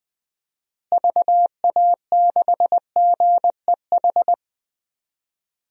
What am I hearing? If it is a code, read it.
Morse code, heard as a keyed tone, about 21 wpm, 700 Hz. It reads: VA6GEH
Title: DXCC-100 Call Signs - 6 Characters - Repeated 3x - 20wpm